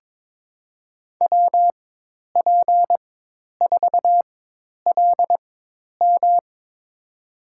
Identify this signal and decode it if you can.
Morse code, heard as a keyed tone, about 22 wpm, spaced as for 10 wpm, 700 Hz. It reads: WP4LM
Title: USA Call Signs - 5 Characters - 4x Character Spacing - 22wpm